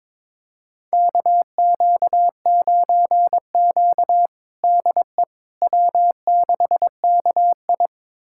Morse code, heard as a keyed tone, about 22 wpm, 700 Hz. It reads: KQ9Q DE W6KI
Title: USA Callsign DE Callsign - 4 Characters - 22wpm